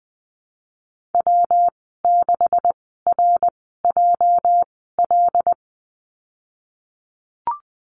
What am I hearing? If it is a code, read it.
Morse code, heard as a keyed tone, about 20 wpm, spaced as for 15 wpm, 700 Hz. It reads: W6RJL
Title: 238 Lesson 38g - Letter q - Call Signs 20wpm